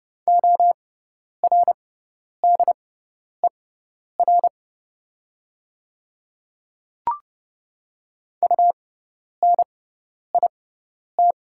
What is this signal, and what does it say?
Morse code, heard as a keyed tone, about 30 wpm, spaced as for 10 wpm, 700 Hz. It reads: ORDER UNIT
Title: 6x Character Spacing - Top 500 Words - 30wpm